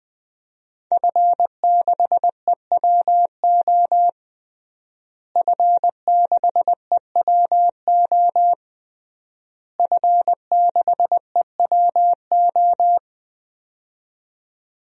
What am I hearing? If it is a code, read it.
Morse code, heard as a keyed tone, about 21 wpm, 700 Hz. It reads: F6EWO F6EWO F6EWO
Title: DXCC-100 Call Signs - 5 Characters - Repeated 3x - 20wpm